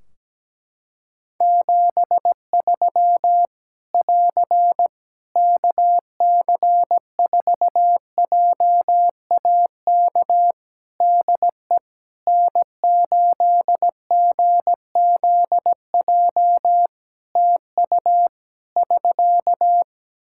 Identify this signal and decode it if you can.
Morse code, heard as a keyed tone, about 17 wpm, 700 Hz. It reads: 73 <AR> KC4JAK DE N8GZJ TU <SK>
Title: Practice QSO - Exchange 4 - Ending - 17wpm